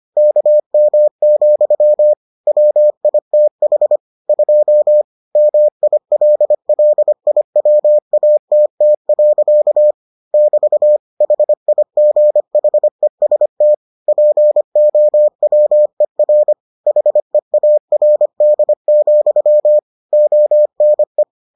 Morse code, heard as a keyed tone, about 25 wpm, 600 Hz. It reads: KM, WITH 2 MILLIWATT. <BT> HIGHEST POWER HEARD, ONE